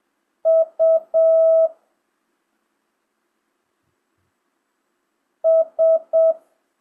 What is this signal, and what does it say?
Morse code, heard as a keyed tone, about 7 wpm, 650 Hz. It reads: U S
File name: jh8cbh / 2025cw